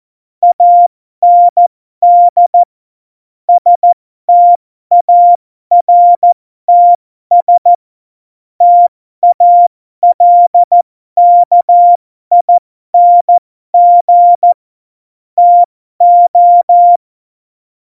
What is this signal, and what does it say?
Morse code, heard as a keyed tone, about 14 wpm, spaced as for 12 wpm, 700 Hz. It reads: AND STARTS TALKING TO
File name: Morsekurs / Witze